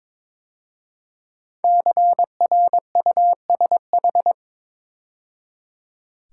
Morse code, heard as a keyed tone, about 22 wpm, 700 Hz. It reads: CRUSH
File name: 5 Letter Words - First 3 Letters Given - 22wpm